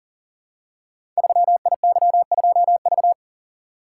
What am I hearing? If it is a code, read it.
Morse code, heard as a keyed tone, about 42 wpm, 700 Hz. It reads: 3IY2V
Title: Code Groups - 5 Letter:Numbers - 40wpm